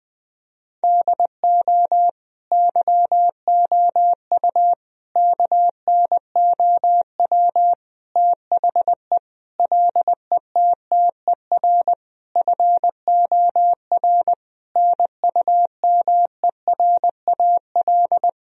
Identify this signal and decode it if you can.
Morse code, heard as a keyed tone, about 20 wpm, 700 Hz. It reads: DO YOU KNOW THE LETTER FOR NUMERAL
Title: Sentences from Top 1000 Words - Review Part 2 - 20wpm